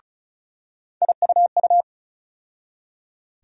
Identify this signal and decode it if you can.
Morse code, heard as a keyed tone, about 34 wpm, 700 Hz. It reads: IUU
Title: USA Call Sign Postfixes - 3 Letters 35wpm